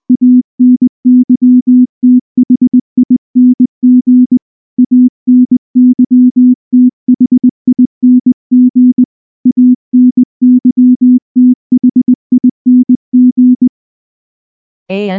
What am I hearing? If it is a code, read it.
Morse code, heard as a keyed tone, about 20 wpm, 250 Hz. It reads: ANYTHING ANYTHING ANYTHING
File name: morse / morse-training-20WPM-6